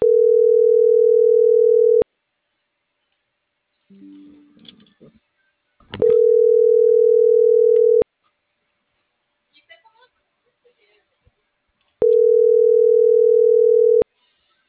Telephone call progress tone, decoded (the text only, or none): ringback tone